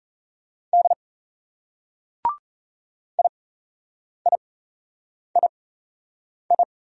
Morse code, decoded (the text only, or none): D IIII